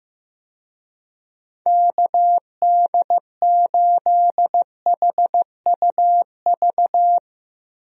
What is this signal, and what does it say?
Morse code, heard as a keyed tone, about 15 wpm, 700 Hz. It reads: KD8HUV
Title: USA Call Signs - 6 Characters - Repeated 2x - 15wpm